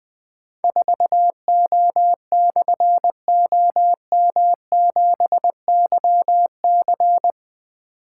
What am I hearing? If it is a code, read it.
Morse code, heard as a keyed tone, about 20 wpm, 700 Hz. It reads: 4O/OM7YC
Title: Difficult Call Signs 20wpm